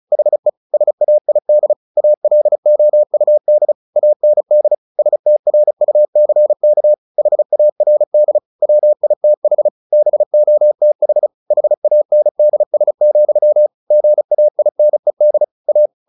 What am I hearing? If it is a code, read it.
Morse code, heard as a keyed tone, about 34 wpm, 600 Hz. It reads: HE SAID ALOUD AND STRUCK HARD WITH BOTH HANDS, GAINED A
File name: the old man and the sea / the old man and the sea 1 35wpm